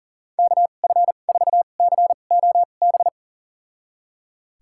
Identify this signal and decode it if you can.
Morse code, heard as a keyed tone, about 38 wpm, spaced as for 28 wpm, 700 Hz. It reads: KF4COB
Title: USA Call Signs - 6 Characters - 2x Character Spacing - 40wpm